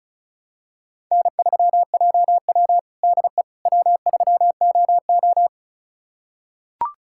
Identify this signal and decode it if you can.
Morse code, heard as a keyed tone, about 34 wpm, 700 Hz. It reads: N3JW DE W3OO
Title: USA Callsign DE Callsign - 4 Characters - 35wpm